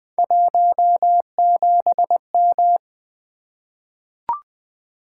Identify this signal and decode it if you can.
Morse code, heard as a keyed tone, about 21 wpm, 700 Hz. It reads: 17M
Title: Three Letter-Number Combinations - 20wpm